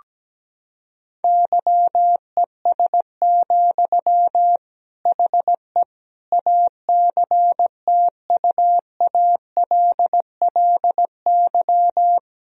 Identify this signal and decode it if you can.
Morse code, heard as a keyed tone, about 17 wpm, 700 Hz. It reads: YES, HE ACTUALLY
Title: Sentences from Top 1000 Words - Review Part 4 - 17wpm